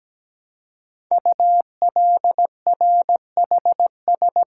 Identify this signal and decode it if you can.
Morse code, heard as a keyed tone, about 17 wpm, 700 Hz. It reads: ULRHS
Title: Code Groups - 5 Letters - 17wpm